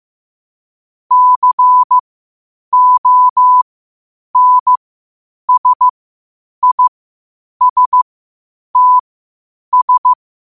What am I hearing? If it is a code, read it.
Morse code, heard as a keyed tone, about 15 wpm, spaced as for 8 wpm, 1000 Hz. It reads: CONSISTS
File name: qotdfiles / 8wpm